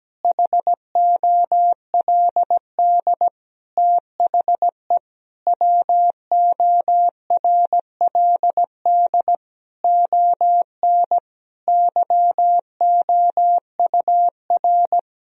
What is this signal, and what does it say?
Morse code, heard as a keyed tone, about 17 wpm, 700 Hz. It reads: HOLD THE WORLD ON YOUR
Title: Sentences from Top 1000 Words - Review Part 2 - 17wpm